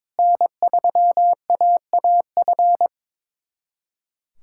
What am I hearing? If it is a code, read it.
Morse code, heard as a keyed tone, about 23 wpm, 700 Hz. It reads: N3AAF
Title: USA Callsign DE Callsign - 5 Characters - 22wpm